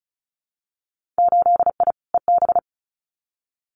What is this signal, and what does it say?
Morse code, heard as a keyed tone, about 36 wpm, 700 Hz. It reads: 8I EB
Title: Winter Field Day Exchanges Encore 35wpm